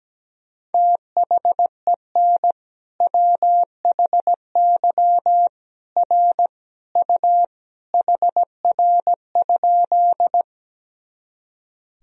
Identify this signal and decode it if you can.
Morse code, heard as a keyed tone, about 17 wpm, 700 Hz. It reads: THEN WHY R U HR?
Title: Sentences from Top 200 Words and Abbreviations 17wpm